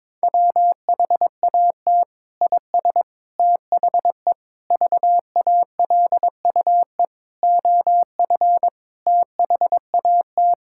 Morse code, heard as a keyed tone, about 22 wpm, 700 Hz. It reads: WHAT IS THE VALUE OF THAT
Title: Sentences from Top 1000 Words - Review Part 6 - 22wpm